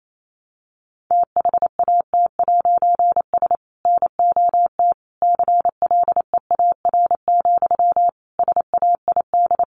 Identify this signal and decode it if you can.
Morse code, heard as a keyed tone, about 28 wpm, 700 Hz. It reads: THAT'S NOT CLEAR, SAID
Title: A Sound Of Thunder 28wpm